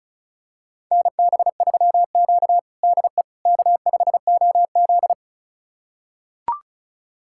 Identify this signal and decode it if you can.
Morse code, heard as a keyed tone, about 34 wpm, 700 Hz. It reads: NB3Q DE K5OZ
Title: USA Callsign DE Callsign - 4 Characters - 35wpm